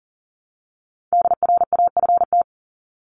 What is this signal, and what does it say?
Morse code, heard as a keyed tone, about 40 wpm, 700 Hz. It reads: DRAFT
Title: Top 1751-2000 Words 40wpm